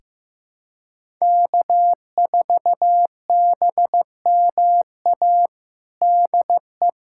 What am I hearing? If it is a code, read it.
Morse code, heard as a keyed tone, about 15 wpm, 700 Hz. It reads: K4BMA DE
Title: USA Callsign DE Callsign - 5 Characters - 15wpm